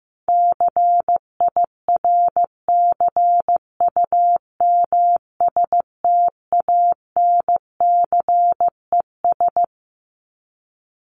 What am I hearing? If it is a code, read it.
Morse code, heard as a keyed tone, about 15 wpm, 700 Hz. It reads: CIRCUMSTANCES
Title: ES:S Suffix - Top 100 Words with Base Words 15wpm